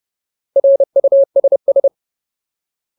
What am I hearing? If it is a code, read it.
Morse code, heard as a keyed tone, about 30 wpm, 550 Hz. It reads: RUSS
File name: Ultimate Word List / 30WPM Ultimate